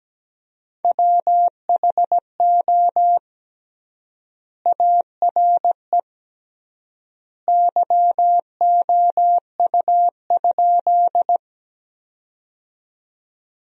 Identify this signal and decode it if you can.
Morse code, heard as a keyed tone, about 17 wpm, 700 Hz. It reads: WHO ARE YOU?
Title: 3x Word-Spacing - Sentences from Top 100 Words - 17wpm